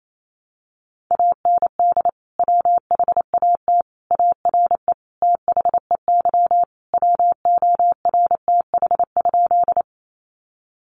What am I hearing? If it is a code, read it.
Morse code, heard as a keyed tone, about 28 wpm, 700 Hz. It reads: AND WHAT ARE THEY WORTH?
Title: Sentences from Top 800 Words 28wpm